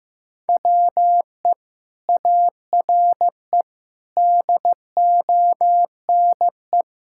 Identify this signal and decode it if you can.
Morse code, heard as a keyed tone, about 16 wpm, 700 Hz. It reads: WE ARE DONE
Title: Sentences from Top 1600 Words 15wpm